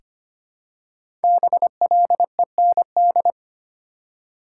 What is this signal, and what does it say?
Morse code, heard as a keyed tone, about 25 wpm, 700 Hz. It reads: BLEND
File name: 5 Letter Words - First 3 Letters Given - 25wpm